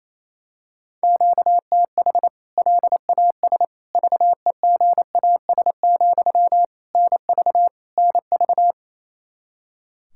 Morse code, heard as a keyed tone, about 27 wpm, 700 Hz. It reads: QTH LAS VEGAS, NV NV
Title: K1USN SST QSO Practice 28wpm